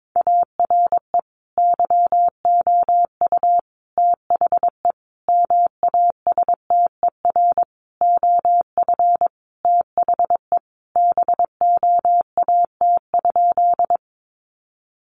Morse code, heard as a keyed tone, about 22 wpm, 700 Hz. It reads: ARE YOU THE MASTER OF THE BOAT?
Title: Sentences from Top 1300 Words 22wpm